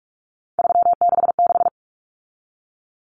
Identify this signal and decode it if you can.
Morse code, heard as a keyed tone, about 44 wpm, 700 Hz. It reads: 366
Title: Numbers - 3-Digit 45wpm